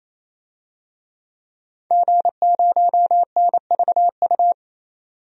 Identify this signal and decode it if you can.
Morse code, heard as a keyed tone, about 27 wpm, 700 Hz. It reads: G0NVU
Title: DXCC-100 Call Signs - 5 Characters - 28wpm